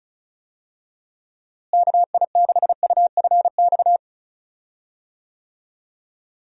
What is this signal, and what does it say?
Morse code, heard as a keyed tone, about 36 wpm, 700 Hz. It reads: KI6UFX
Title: USA Call Signs - 6 Characters - Repeated 2x - 35wpm